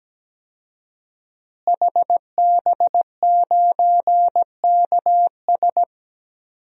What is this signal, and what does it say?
Morse code, heard as a keyed tone, about 17 wpm, 700 Hz. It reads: HB9KS
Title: DXCC-100 Call Signs - 5 Characters - Repeated 2x - 17wpm